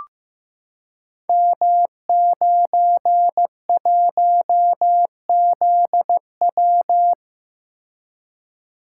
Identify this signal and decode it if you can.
Morse code, heard as a keyed tone, about 15 wpm, 700 Hz. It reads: M91ZW
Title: Code Groups - 5 Letter:Numbers - 15wpm